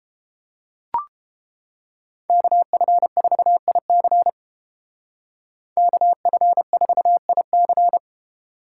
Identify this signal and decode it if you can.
Morse code, heard as a keyed tone, about 29 wpm, 700 Hz. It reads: KF4IC KF4IC